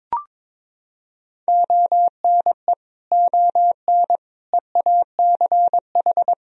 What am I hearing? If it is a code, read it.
Morse code, heard as a keyed tone, about 21 wpm, 700 Hz. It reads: ONE ON EACH
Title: Sentences from Top 1000 Words - Review Part 2 - 22wpm